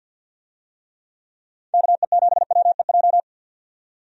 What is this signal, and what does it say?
Morse code, heard as a keyed tone, about 52 wpm, 700 Hz. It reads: KE7WEJ
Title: USA Call Signs - 6 Characters - 50wpm